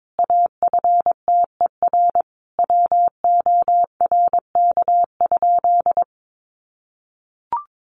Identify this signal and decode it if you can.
Morse code, heard as a keyed tone, about 22 wpm, 700 Hz. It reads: AFTER WORK?
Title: Sentences from Top 1400 Words 22wpm